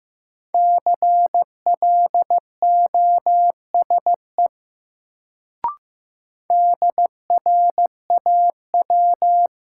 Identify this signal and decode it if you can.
Morse code, heard as a keyed tone, about 15 wpm, 700 Hz. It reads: CLOSE DRAW
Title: Sentences from Top 300 Words - New Word Drill - 15wpm